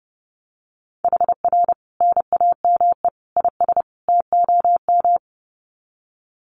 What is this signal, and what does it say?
Morse code, heard as a keyed tone, about 30 wpm, 700 Hz. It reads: HR NAME IS TOM
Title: Common QSO Phrases 30wpm